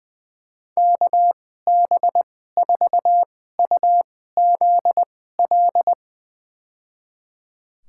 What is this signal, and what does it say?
Morse code, heard as a keyed tone, about 21 wpm, spaced as for 15 wpm, 700 Hz. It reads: KB4UZL
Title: USA Call Signs - 6 Characters - 2x Character Spacing - 20wpm